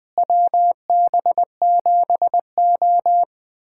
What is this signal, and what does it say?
Morse code, heard as a keyed tone, about 20 wpm, 700 Hz. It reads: WB7O
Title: USA Callsign DE Callsign - 4 Characters - 20wpm